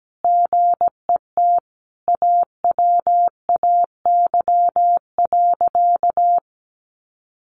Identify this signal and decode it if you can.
Morse code, heard as a keyed tone, about 17 wpm, 700 Hz. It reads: GET AWAY.
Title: Sentences from Top 600 Words 17wpm